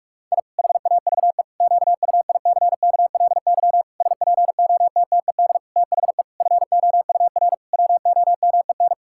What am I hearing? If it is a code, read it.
Morse code, heard as a keyed tone, about 41 wpm, 700 Hz. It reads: I HAVE QUICKLY SPOTTED THE FOUR WOMEN
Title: Pangram Challenge v2 - 45wpm